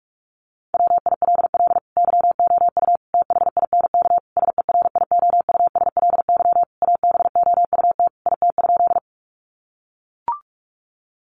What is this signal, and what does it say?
Morse code, heard as a keyed tone, about 44 wpm, 700 Hz. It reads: WILL YOU THINK SERIOUSLY ABOUT IT?
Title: Sentences from Top 1200 Words 45wpm